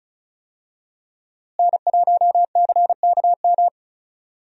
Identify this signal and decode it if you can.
Morse code, heard as a keyed tone, about 34 wpm, 700 Hz. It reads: N1CKM
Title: USA Call Signs - 5 Characters - Repeated 3x - 35wpm